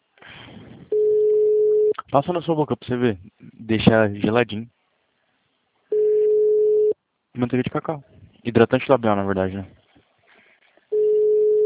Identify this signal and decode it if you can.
Ringback tone (Europe, 425 Hz), with speech.